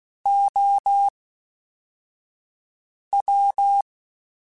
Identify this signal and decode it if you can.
Morse code, heard as a keyed tone, about 16 wpm, 800 Hz. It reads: O W